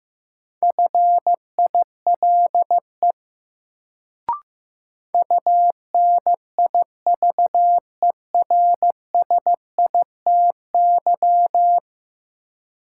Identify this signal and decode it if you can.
Morse code, heard as a keyed tone, about 15 wpm, 700 Hz. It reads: FILE UNIVERSITY
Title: Sentences from Top 1000 Words - Review Part 4 - New Word Drill - 15wpm